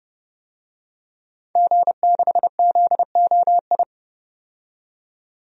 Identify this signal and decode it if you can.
Morse code, heard as a keyed tone, about 31 wpm, 700 Hz. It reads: G6ZOI
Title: DXCC-100 Call Signs - 5 Characters - 30wpm